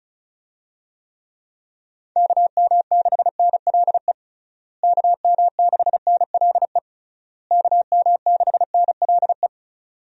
Morse code, heard as a keyed tone, about 36 wpm, 700 Hz. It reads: KM6NLE KM6NLE KM6NLE
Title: USA Call Signs - 6 Characters - Repeated 3x - 35wpm